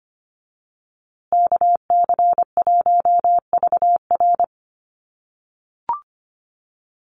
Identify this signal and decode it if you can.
Morse code, heard as a keyed tone, about 25 wpm, 700 Hz. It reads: KC1VR
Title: USA Call Signs Repeated 3x 25wpm